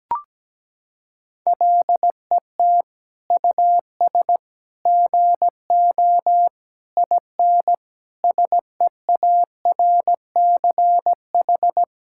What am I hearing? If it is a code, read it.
Morse code, heard as a keyed tone, about 17 wpm, 700 Hz. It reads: LET US GO IN SEARCH
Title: Sentences from Top 1000 Words - Review Part 5 - 17wpm